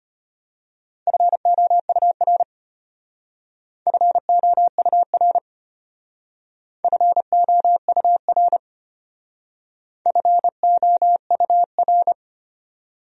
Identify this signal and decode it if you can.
Morse code, heard as a keyed tone, about 32 wpm, 700 Hz. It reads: FOUR FOUR FOUR FOUR